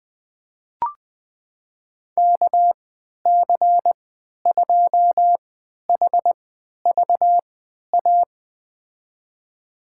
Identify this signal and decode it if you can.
Morse code, heard as a keyed tone, about 20 wpm, spaced as for 11 wpm, 700 Hz. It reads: KC2HVA
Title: USA Call Signs - 6 Characters - 3x Character Spacing - 20wpm